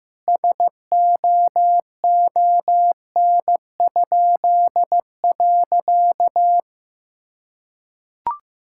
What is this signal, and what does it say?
Morse code, heard as a keyed tone, about 15 wpm, 700 Hz. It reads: SOON?.
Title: Pangram Challenge v2 - 15wpm